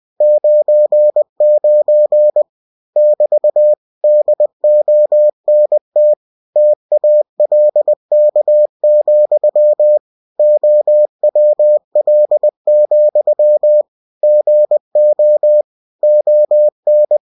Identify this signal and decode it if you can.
Morse code, heard as a keyed tone, about 20 wpm, 600 Hz. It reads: 99 <BT> DONT TALK, OWL, GO ON
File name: cw-books / winnie-the-pooh / winnie-20wpm-page-0099